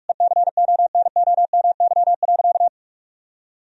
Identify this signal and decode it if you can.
Morse code, heard as a keyed tone, about 46 wpm, 700 Hz. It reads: ECONOMY.
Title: Sentences from Top 1000 Words - Review Part 3 - 45wpm